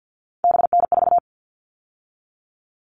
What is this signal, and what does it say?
Morse code, heard as a keyed tone, about 48 wpm, 700 Hz. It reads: BN4
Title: All Cut Numbers - 3-Digit 50wpm